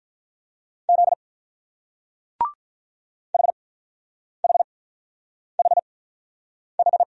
Morse code, heard as a keyed tone, about 43 wpm, spaced as for 9 wpm, 700 Hz. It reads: Z HHHH
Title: Speed-Racing Single Letters 35wpm